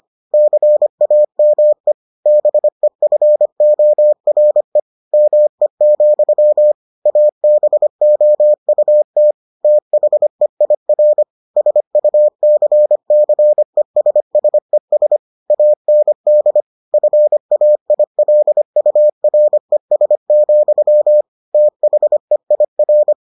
Morse code, heard as a keyed tone, about 25 wpm, 600 Hz. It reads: CAME BEFORE ME, ABOUT THEIR SUCCESSES AND FAILURES, THEIR